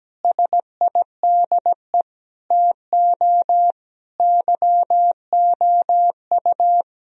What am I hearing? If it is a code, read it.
Morse code, heard as a keyed tone, about 17 wpm, 700 Hz. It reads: SIDE TO YOU
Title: Sentences from Top 1000 Words - Review Part 3 - 17wpm